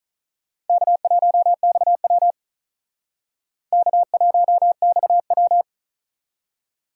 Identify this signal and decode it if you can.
Morse code, heard as a keyed tone, about 39 wpm, 700 Hz. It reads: K1XW K1XW
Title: Speed Racing - USA 1x2 Call Signs 35wpm